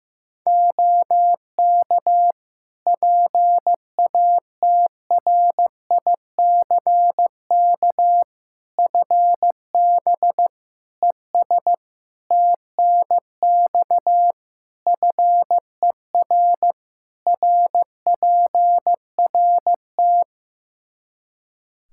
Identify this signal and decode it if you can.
Morse code, heard as a keyed tone, about 15 wpm, 700 Hz. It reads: OK PATRICK FB ES TNX FER RPRT
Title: Practice QSO - Exchange 2 - Info - 15wpm